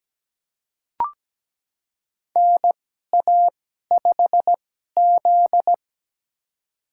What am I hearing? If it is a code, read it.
Morse code, heard as a keyed tone, about 17 wpm, spaced as for 12 wpm, 700 Hz. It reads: NA5Z
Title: USA Call Signs - 4 Characters - 2x Character Spacing - 17wpm